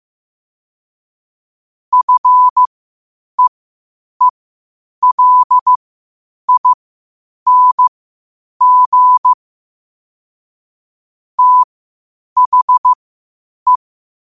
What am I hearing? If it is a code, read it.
Morse code, heard as a keyed tone, about 15 wpm, spaced as for 8 wpm, 1000 Hz. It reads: FEELING THE
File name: qotdfiles / 8wpm